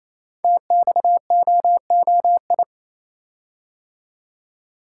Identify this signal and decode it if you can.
Morse code, heard as a keyed tone, about 27 wpm, 700 Hz. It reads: TXOOI
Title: Code Groups - 5 Letters - 28wpm